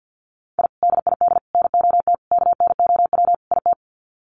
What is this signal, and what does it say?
Morse code, heard as a keyed tone, about 50 wpm, 700 Hz. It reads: I DID NOT KNOW IT
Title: Sets of 5 Words Encore2 50wpm